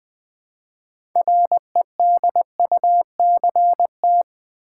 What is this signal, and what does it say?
Morse code, heard as a keyed tone, about 20 wpm, 700 Hz. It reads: REDUCT
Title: Build A Word - 3 Letters At A Time - 9-Letter Words - 20wpm